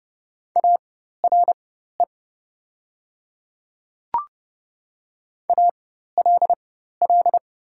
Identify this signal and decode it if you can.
Morse code, heard as a keyed tone, about 30 wpm, spaced as for 14 wpm, 700 Hz. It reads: ARE ALL
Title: 4x Character Spacing - Top 100 Words - 30wpm